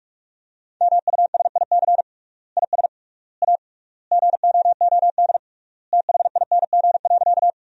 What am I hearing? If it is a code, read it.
Morse code, heard as a keyed tone, about 47 wpm, 700 Hz. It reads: MUSIC IS A GOOD THING.
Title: 3x Word-Spacing - Sentences from Top 300 Words - 45wpm